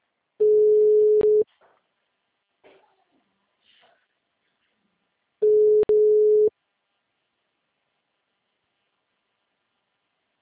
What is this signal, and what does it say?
Ringback tone (Europe, 425 Hz).